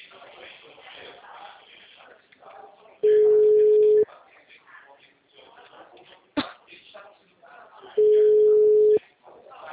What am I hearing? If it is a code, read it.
Ringback tone (Europe, 425 Hz), with speech.